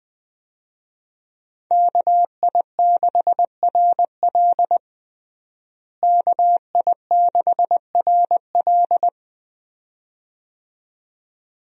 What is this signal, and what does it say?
Morse code, heard as a keyed tone, about 20 wpm, 700 Hz. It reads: KI6RL KI6RL
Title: USA Call Signs - 5 Characters - Repeated 2x - 20wpm